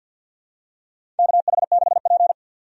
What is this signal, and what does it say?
Morse code, heard as a keyed tone, about 51 wpm, 700 Hz. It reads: KH6P
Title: DXCC-100 Call Signs - 4 Characters - 50wpm